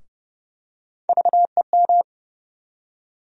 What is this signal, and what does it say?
Morse code, heard as a keyed tone, about 30 wpm, 700 Hz. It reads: VEM
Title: USA Call Sign Postfixes - 3 Letters 30wpm